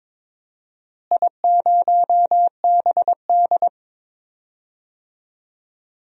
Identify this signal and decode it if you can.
Morse code, heard as a keyed tone, about 22 wpm, 700 Hz. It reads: I0BD
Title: DXCC-100 Call Signs - 4 Characters - Repeated 2x - 22wpm